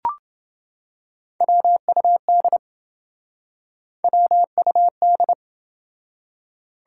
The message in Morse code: WUD WUD

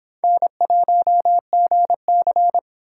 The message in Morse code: N1GC

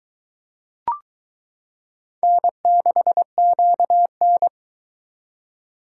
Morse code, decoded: N6QN

